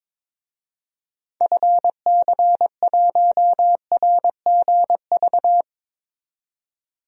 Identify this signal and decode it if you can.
Morse code, heard as a keyed tone, about 22 wpm, 700 Hz. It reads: FC1RGV
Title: DXCC-100 Call Signs - 6 Characters - 22wpm